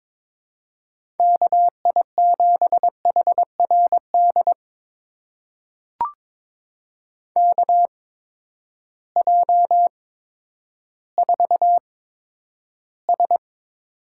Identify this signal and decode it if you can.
Morse code, heard as a keyed tone, about 21 wpm, spaced as for 11 wpm, 700 Hz. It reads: KI7HRD KJ4S
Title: USA Call Signs - 6 Characters - 8x Character Spacing - 22wpm